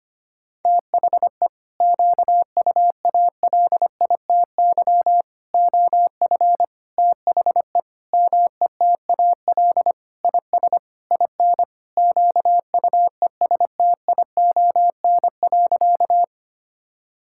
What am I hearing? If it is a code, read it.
Morse code, heard as a keyed tone, about 25 wpm, 700 Hz. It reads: THE QUALITY OF THE METAL IS IN QUESTION.
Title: Sentences from Top 1000 Words - Review Part 6 - 25wpm